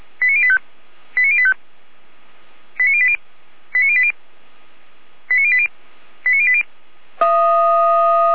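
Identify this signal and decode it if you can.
Five-tone selective call (ZVEI), heard as quick runs of five tones, tones of about 70 ms. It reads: 89085 89085 89080 89080 89080 89080
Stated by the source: ZVEI five-tone call 89092 received on SDR.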